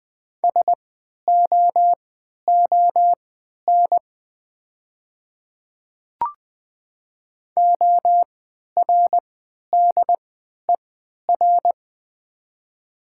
Morse code, heard as a keyed tone, about 20 wpm, spaced as for 11 wpm, 700 Hz. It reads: SOON ORDER
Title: 3x Character Spacing - Top 500 Words - 20wpm